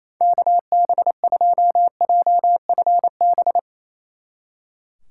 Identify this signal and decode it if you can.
Morse code, heard as a keyed tone, about 34 wpm, 700 Hz. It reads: KB2JFB
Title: Speed Racing - USA 2x3 Call Signs 28wpm